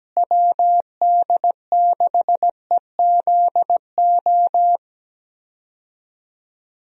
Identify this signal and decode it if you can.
Morse code, heard as a keyed tone, about 17 wpm, 700 Hz. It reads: WD6EZO
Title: USA Call Signs - 6 Characters - Repeated 3x - 17wpm